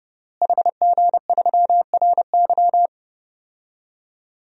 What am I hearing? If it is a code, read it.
Morse code, heard as a keyed tone, about 30 wpm, 700 Hz. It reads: HG3RY
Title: DXCC-100 Call Signs - 5 Characters - Repeated 3x - 30wpm